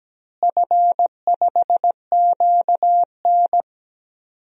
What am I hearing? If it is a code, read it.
Morse code, heard as a keyed tone, about 17 wpm, 700 Hz. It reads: F5QN
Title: DXCC-100 Call Signs - 4 Characters - Repeated 3x - 17wpm